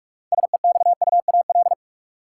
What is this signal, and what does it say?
Morse code, heard as a keyed tone, about 42 wpm, 700 Hz. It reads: SEXUAL
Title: Six Letter Words - 45wpm